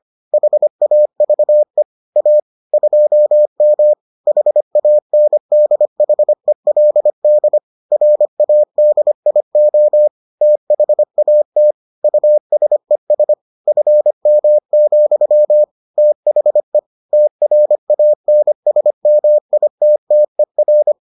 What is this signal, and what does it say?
Morse code, heard as a keyed tone, about 25 wpm, 600 Hz. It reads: HAVE A 2M HANDHELD RADIO THAT USES FM, THE TRANSMITTER